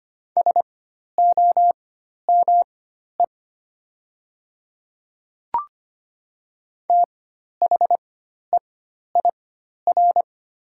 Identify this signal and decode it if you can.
Morse code, heard as a keyed tone, about 25 wpm, spaced as for 12 wpm, 690 Hz. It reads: SOME THEIR
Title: 4x Character Spacing - Top 200 Words - 25wpm